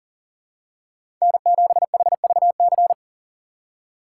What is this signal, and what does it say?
Morse code, heard as a keyed tone, about 40 wpm, 700 Hz. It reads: N7HVC
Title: USA Call Signs - 5 Characters - 4x Character Spacing - 40wpm